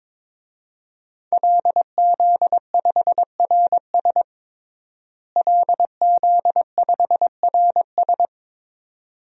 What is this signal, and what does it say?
Morse code, heard as a keyed tone, about 22 wpm, 700 Hz. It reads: LZ5RS LZ5RS
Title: DXCC-100 Call Signs - 5 Characters - Repeated 2x - 22wpm